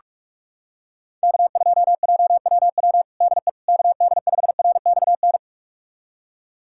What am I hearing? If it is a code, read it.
Morse code, heard as a keyed tone, about 46 wpm, 700 Hz. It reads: K2JWW DE KD5RXN
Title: USA Callsign DE Callsign 45wpm